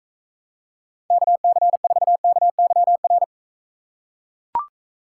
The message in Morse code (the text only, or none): KC4KYR